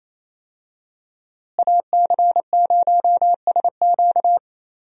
Morse code, heard as a keyed tone, about 28 wpm, 700 Hz. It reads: AC0SQ
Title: USA Call Signs - 5 Characters - Repeated 3x - 28wpm